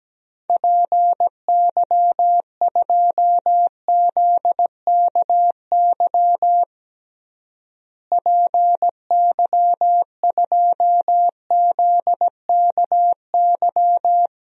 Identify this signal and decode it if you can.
Morse code, heard as a keyed tone, about 17 wpm, 700 Hz. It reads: PY2ZKY PY2ZKY
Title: DXCC-100 Call Signs - 6 Characters - Repeated 3x - 17wpm